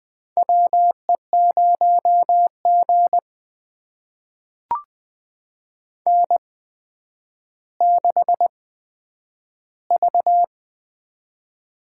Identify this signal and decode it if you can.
Morse code, heard as a keyed tone, about 20 wpm, spaced as for 10 wpm, 700 Hz. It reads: WE0G N6V
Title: USA Call Signs - 8x Character Spacing - 20wpm